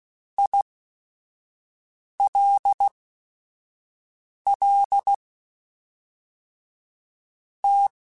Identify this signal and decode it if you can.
Morse code, heard as a keyed tone, about 16 wpm, spaced as for 5 wpm, 800 Hz. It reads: ILL T